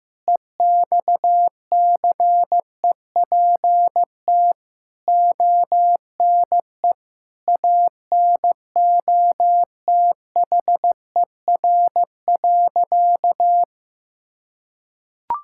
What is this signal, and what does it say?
Morse code, heard as a keyed tone, about 15 wpm, 700 Hz. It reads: EXCEPT ONE ANOTHER.
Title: Sentences from Top 1600 Words 15wpm